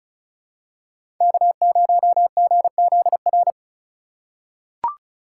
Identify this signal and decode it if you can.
Morse code, heard as a keyed tone, about 34 wpm, 700 Hz. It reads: K0GZR
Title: USA Call Signs - 5 Characters - 3x Character Spacing - 35wpm